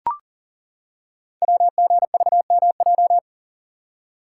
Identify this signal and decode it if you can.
Morse code, heard as a keyed tone, about 38 wpm, 700 Hz. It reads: WGVMJ
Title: Code Groups - 5 Letters - 40wpm